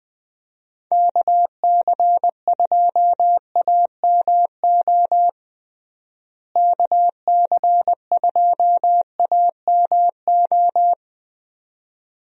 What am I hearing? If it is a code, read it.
Morse code, heard as a keyed tone, about 20 wpm, 700 Hz. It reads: KC2AMO KC2AMO
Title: USA Call Signs - 6 Characters - Repeated 2x - 20wpm